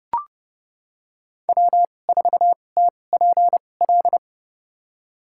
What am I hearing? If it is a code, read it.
Morse code, heard as a keyed tone, about 30 wpm, spaced as for 22 wpm, 700 Hz. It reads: W4TPL
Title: USA Call Signs - 5 Characters - 2x Character Spacing - 30wpm